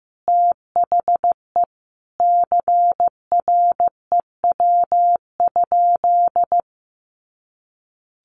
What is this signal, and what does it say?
Morse code, heard as a keyed tone, about 14 wpm, 700 Hz. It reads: THE CREW?
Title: Sentences from Top 1500 Words 15wpm